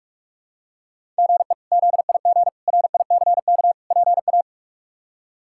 Morse code, heard as a keyed tone, about 46 wpm, 700 Hz. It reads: GE ZIG RICK PA
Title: K1USN SST QSO Practice - 45wpm